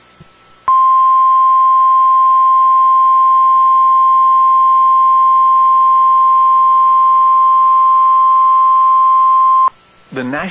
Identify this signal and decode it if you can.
Emergency Alert System (SAME) tones (the 1050 Hz attention tone), with speech. It is heard from 0.6 to 9.7 s.